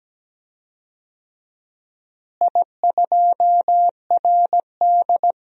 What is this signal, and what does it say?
Morse code, heard as a keyed tone, about 17 wpm, 700 Hz. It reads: I2RD
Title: DXCC-100 Call Signs - 4 Characters - Repeated 3x - 17wpm